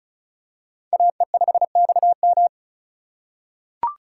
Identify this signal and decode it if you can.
Morse code, heard as a keyed tone, about 36 wpm, 700 Hz. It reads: AE5XM
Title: USA Call Signs - 5 Characters - 4x Character Spacing - 35wpm